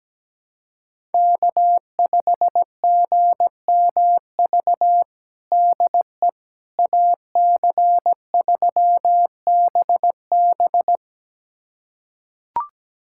Morse code, heard as a keyed tone, about 17 wpm, 700 Hz. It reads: K5GMV DE AC3BB
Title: USA Callsign DE Callsign - 5 Characters - 17wpm